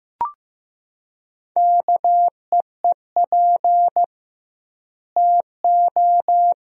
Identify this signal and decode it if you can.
Morse code, heard as a keyed tone, about 15 wpm, 700 Hz. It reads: KEEP TO
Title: 2x Word-Spacing - Sentences from Top 300 Words - 15wpm